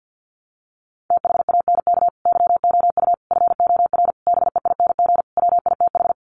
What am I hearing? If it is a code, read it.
Morse code, heard as a keyed tone, about 49 wpm, 700 Hz. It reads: THANK YOU FOR BEING WITH
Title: Sets of 5 Words 50wpm